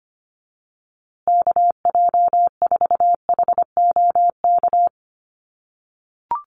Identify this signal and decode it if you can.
Morse code, heard as a keyed tone, about 25 wpm, 700 Hz. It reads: KJ4HOK
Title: USA Call Signs Repeated 2x 25wpm